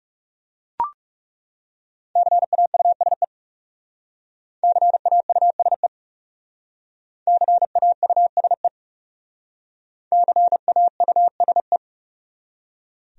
Morse code, heard as a keyed tone, about 36 wpm, 700 Hz. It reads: CAUSE CAUSE CAUSE CAUSE
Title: Speed-Racing Top 200 Words 30wpm